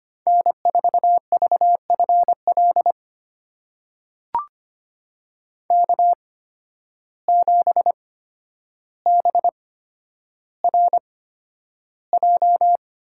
Morse code, heard as a keyed tone, about 25 wpm, spaced as for 11 wpm, 700 Hz. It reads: N4VFL K7BRJ
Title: USA Call Signs - 5 Characters - 8x Character Spacing - 25wpm